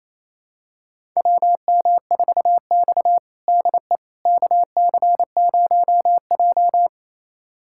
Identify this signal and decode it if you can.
Morse code, heard as a keyed tone, about 28 wpm, 700 Hz. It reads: WM4X DE KC0J
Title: USA Callsign DE Callsign - 4 Characters - 28wpm